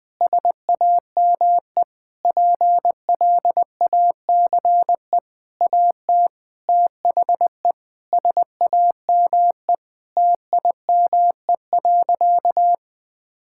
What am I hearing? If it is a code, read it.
Morse code, heard as a keyed tone, about 20 wpm, 700 Hz. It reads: SAME PLACE AT THE SAME TIME.
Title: Sentences with 8-Word Phrases - 20wpm